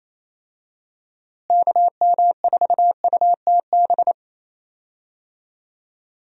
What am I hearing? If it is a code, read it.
Morse code, heard as a keyed tone, about 28 wpm, 700 Hz. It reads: KM4UTB
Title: USA Call Signs - 6 Characters - Repeated 2x - 28wpm